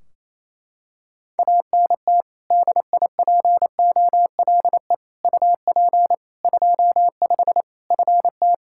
Morse code, heard as a keyed tone, about 28 wpm, 700 Hz. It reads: ANT DIPOLE UP 25 FT
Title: K1USN SST QSO Practice 28wpm